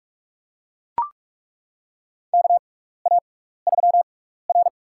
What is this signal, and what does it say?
Morse code, heard as a keyed tone, about 42 wpm, spaced as for 15 wpm, 700 Hz. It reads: KA3R
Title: USA Call Signs - 4 Characters - 6x Character Spacing - 45wpm